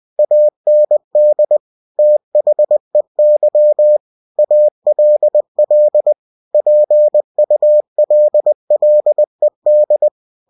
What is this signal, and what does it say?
Morse code, heard as a keyed tone, about 20 wpm, 600 Hz. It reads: AND THEY ALL PULLED
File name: cw-books / winnie-the-pooh / winnie-20wpm-page-0019